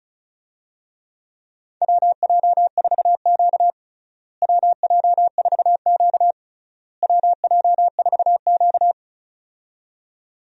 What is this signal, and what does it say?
Morse code, heard as a keyed tone, about 34 wpm, 700 Hz. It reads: WJ4Q WJ4Q WJ4Q
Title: USA Call Signs - 4 Characters - Repeated 3x - 35wpm